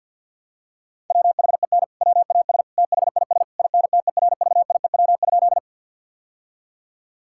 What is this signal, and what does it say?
Morse code, heard as a keyed tone, about 50 wpm, 700 Hz. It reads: WHEN WAS THIS INTERVIEW?
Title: Sentences from Top 1600 Words 50wpm